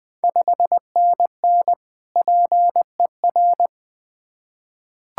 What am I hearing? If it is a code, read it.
Morse code, heard as a keyed tone, about 21 wpm, 700 Hz. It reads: 5NN PER
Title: Ontario QSO Party Exchange 20wpm